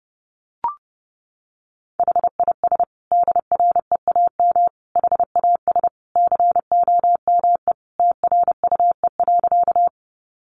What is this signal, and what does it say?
Morse code, heard as a keyed tone, about 28 wpm, 700 Hz. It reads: HIS DREAM HAS COME TRUE.
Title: Sentences from Top 1500 Words 30wpm